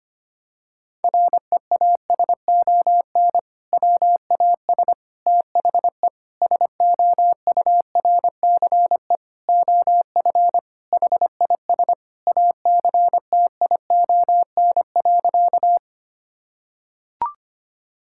Morse code, heard as a keyed tone, about 25 wpm, 700 Hz. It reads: REASON WAS THE SOURCE OF HIS ACTION.
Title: Sentences from Top 1000 Words - Review Part 5 - 25wpm